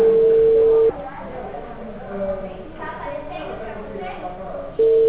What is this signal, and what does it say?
Ringback tone (China, 450 Hz), with speech.
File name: out- -2012- - - .57162